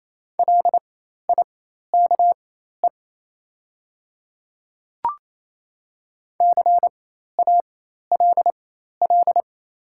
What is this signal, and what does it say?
Morse code, heard as a keyed tone, about 28 wpm, spaced as for 13 wpm, 700 Hz. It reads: LIKE CALL
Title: 4x Character Spacing - Top 200 Words - 28wpm